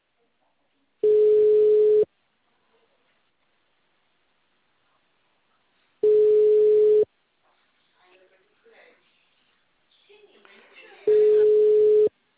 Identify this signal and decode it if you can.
Ringback tone (Europe, 425 Hz), with speech.
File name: out- -2012- - - .3391